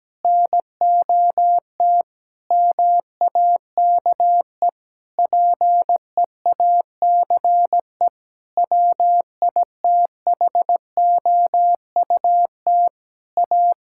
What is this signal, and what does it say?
Morse code, heard as a keyed tone, about 17 wpm, 700 Hz. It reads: NOT MAKE PEACE WITHOUT A
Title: Sentences from Top 1600 Words 17wpm